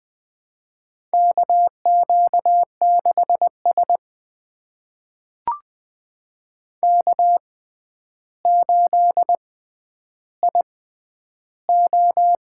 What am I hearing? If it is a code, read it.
Morse code, heard as a keyed tone, about 20 wpm, spaced as for 10 wpm, 700 Hz. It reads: KQ6S K8IO
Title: USA Call Signs - 4 Characters - 6x Character Spacing - 20wpm